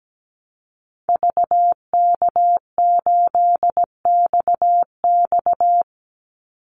Morse code, heard as a keyed tone, about 17 wpm, 700 Hz. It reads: VK8XX
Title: Call Signs - 17wpm